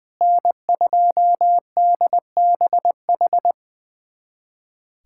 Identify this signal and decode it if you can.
Morse code, heard as a keyed tone, about 20 wpm, 700 Hz. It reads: N2DBH
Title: USA Callsign DE Callsign - 5 Characters - 20wpm